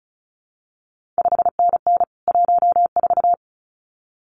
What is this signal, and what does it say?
Morse code, heard as a keyed tone, about 36 wpm, 700 Hz. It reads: 5NN 14
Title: CQ WW DX Contest Exchange - 35wpm